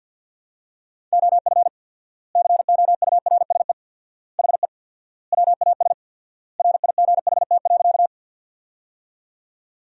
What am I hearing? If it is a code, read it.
Morse code, heard as a keyed tone, about 50 wpm, 700 Hz. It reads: OF COURSE HE WAS RIGHT.
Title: 4x Word-Spacing - Sentences from Top 500 Words - 50wpm